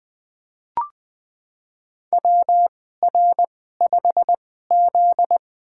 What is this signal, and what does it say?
Morse code, heard as a keyed tone, about 20 wpm, spaced as for 14 wpm, 700 Hz. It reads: WR5Z
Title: USA Call Signs - 2x Character Spacing - 20wpm